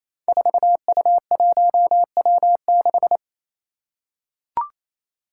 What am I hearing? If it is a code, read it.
Morse code, heard as a keyed tone, about 28 wpm, 700 Hz. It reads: 4U1W6
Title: Five Challenging Letter-Number Combinations 28wpm